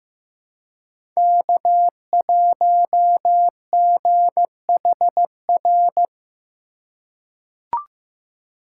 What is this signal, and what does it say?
Morse code, heard as a keyed tone, about 15 wpm, 700 Hz. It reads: K1GHR
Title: USA Call Signs - 5 Characters - 4x Character Spacing - 15wpm